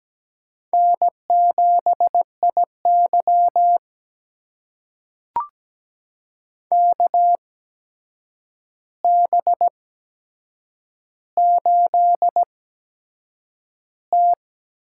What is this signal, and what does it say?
Morse code, heard as a keyed tone, about 17 wpm, spaced as for 7 wpm, 700 Hz. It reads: N7IY KB8T
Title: USA Call Signs - 4 Characters - 8x Character Spacing - 17wpm